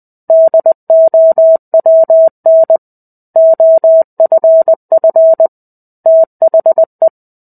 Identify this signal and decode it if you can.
Morse code, heard as a keyed tone, about 20 wpm, 650 Hz. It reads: DOWN OFF THE
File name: 05-Chapter-05 20 WPM